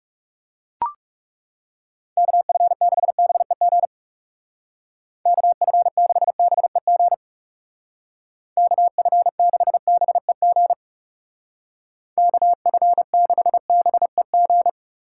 Morse code, heard as a keyed tone, about 38 wpm, 700 Hz. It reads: KF6BEG KF6BEG KF6BEG KF6BEG